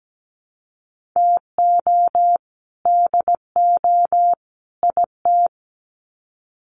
Morse code, heard as a keyed tone, about 17 wpm, 700 Hz. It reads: TO DO IT
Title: Sets of 3 Words 17wpm